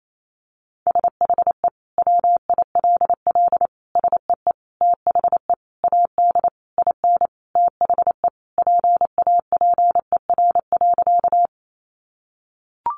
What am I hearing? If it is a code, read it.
Morse code, heard as a keyed tone, about 28 wpm, 700 Hz. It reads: SHE WILL SEE THE AD IN THE PAPER.
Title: Sentences from Top 1300 Words 28wpm